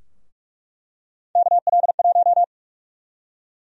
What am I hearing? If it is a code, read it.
Morse code, heard as a keyed tone, about 44 wpm, 700 Hz. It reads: KL1
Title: USA Call Sign Prefixes - 45wpm